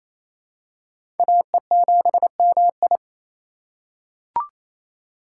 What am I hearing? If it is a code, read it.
Morse code, heard as a keyed tone, about 28 wpm, 700 Hz. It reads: AE7MI
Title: USA Call Signs - 5 Characters - 2x Character Spacing - 28wpm